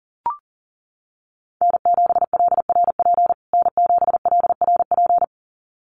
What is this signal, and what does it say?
Morse code, heard as a keyed tone, about 40 wpm, 700 Hz. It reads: N7LRP N7LRP
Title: USA Call Signs Repeated 2x 40wpm